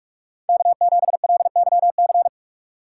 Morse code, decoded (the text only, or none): K7LYC